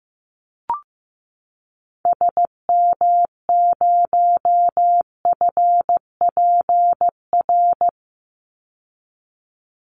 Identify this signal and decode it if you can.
Morse code, heard as a keyed tone, about 15 wpm, 700 Hz. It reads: SM0FPR
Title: Call Signs Repeated 2x - 15wpm